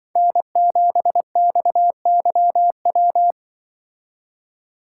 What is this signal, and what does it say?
Morse code, heard as a keyed tone, about 25 wpm, 700 Hz. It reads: N7XYW
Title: Speed Racing - USA 1x3 Call Signs 20wpm